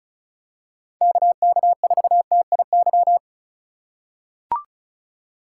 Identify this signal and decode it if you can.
Morse code, heard as a keyed tone, about 36 wpm, 700 Hz. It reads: KK4TIY
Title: USA Call Signs - 6 Characters - 4x Character Spacing - 35wpm